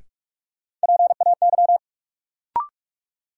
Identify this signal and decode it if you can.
Morse code, heard as a keyed tone, about 46 wpm, 700 Hz. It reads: PAY
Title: Sentences from Top 700 Words - New Word Drill - 45wpm